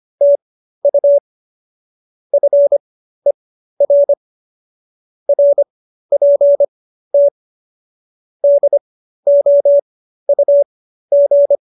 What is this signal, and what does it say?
Morse code, heard as a keyed tone, about 25 wpm, spaced as for 13 wpm, 570 Hz. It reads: TU FER RPT DOUG